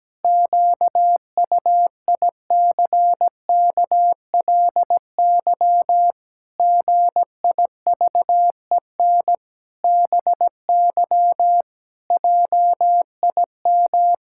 Morse code, heard as a keyed tone, about 17 wpm, 700 Hz. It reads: QUICKLY GIVEN BY JIM
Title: Pangram Challenge v2 - 17wpm